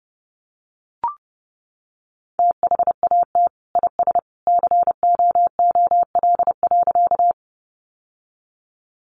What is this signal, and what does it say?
Morse code, heard as a keyed tone, about 30 wpm, 700 Hz. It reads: THAT IS COOL.
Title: Sentences from Top 1200 Words 30wpm